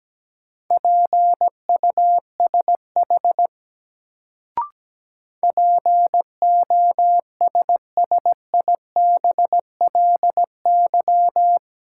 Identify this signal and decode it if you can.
Morse code, heard as a keyed tone, about 17 wpm, 700 Hz. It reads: PUSH POSSIBLY
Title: Sentences from Top 1300 Words - New Word Drill Alt - 17wpm